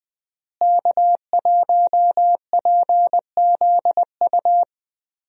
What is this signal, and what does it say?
Morse code, heard as a keyed tone, about 21 wpm, 700 Hz. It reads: K1PZU
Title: USA Call Signs - 5 Characters - Repeated 2x - 20wpm